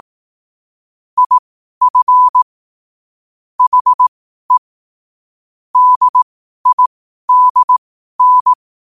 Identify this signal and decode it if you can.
Morse code, heard as a keyed tone, about 18 wpm, spaced as for 12 wpm, 1000 Hz. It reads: IF HE DIDN
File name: qotdfiles / 12wpm